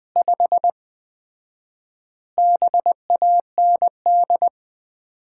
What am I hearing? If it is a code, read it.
Morse code, heard as a keyed tone, about 20 wpm, 700 Hz. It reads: 5 BAND
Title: Common QSO Phrases - 4x Word-Spacing - 20wpm